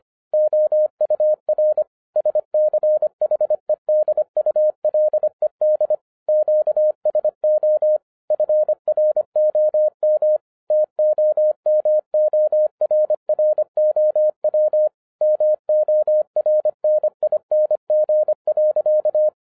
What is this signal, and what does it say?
Morse code, heard as a keyed tone, about 25 wpm, 610 Hz. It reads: OUR SCHEDULED QSO FROM TOMORROW MORNING.